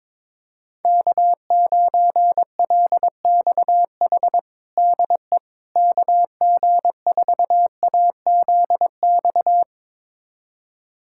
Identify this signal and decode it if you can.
Morse code, heard as a keyed tone, about 22 wpm, 700 Hz. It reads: K9LXH DE KG4AZX
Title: USA Callsign DE Callsign 22wpm